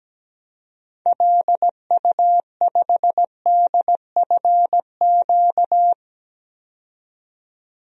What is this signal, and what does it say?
Morse code, heard as a keyed tone, about 17 wpm, 700 Hz. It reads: LU5DFQ
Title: DXCC-100 Call Signs - 6 Characters - 17wpm